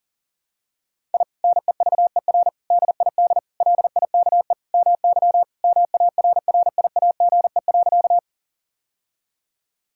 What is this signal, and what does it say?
Morse code, heard as a keyed tone, about 40 wpm, 700 Hz. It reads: I NEVER DID LIKE MY MARRIAGE.
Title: Sentences from Top 1000 Words - Review Part 5 - 40wpm